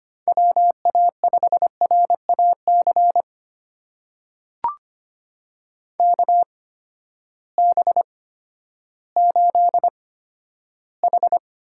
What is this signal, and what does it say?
Morse code, heard as a keyed tone, about 25 wpm, spaced as for 12 wpm, 700 Hz. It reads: WA5RAC KB8H
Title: USA Call Signs - 6 Characters - 8x Character Spacing - 25wpm